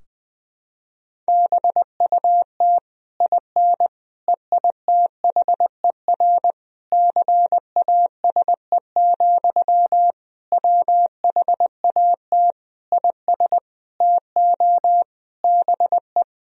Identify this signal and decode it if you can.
Morse code, heard as a keyed tone, about 20 wpm, 700 Hz. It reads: BUT IN EITHER CASE, WHAT IS TO BE
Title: Sentences from Top 1000 Words - Review Part 4 - 20wpm